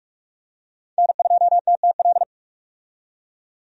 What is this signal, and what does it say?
Morse code, heard as a keyed tone, about 46 wpm, 700 Hz. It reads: N2TTL
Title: USA Call Signs - 5 Characters - 45wpm